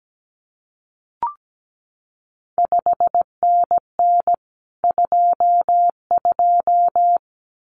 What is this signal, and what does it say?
Morse code, heard as a keyed tone, about 17 wpm, 700 Hz. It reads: 5NN 22
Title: CQ WW DX Contest Exchange Encore - 17wpm